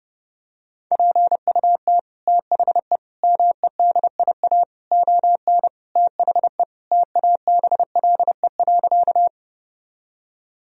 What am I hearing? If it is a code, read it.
Morse code, heard as a keyed tone, about 30 wpm, 700 Hz. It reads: PUT THE MEDIA ON THE TABLE.
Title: Sentences from Top 1000 Words - Review Part 4 - 30wpm